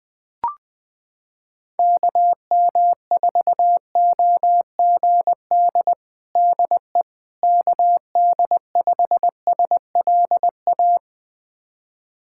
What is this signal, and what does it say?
Morse code, heard as a keyed tone, about 20 wpm, 700 Hz. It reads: KM4OGD DE KD5SLA
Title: USA Callsign DE Callsign - 6 Characters - 20wpm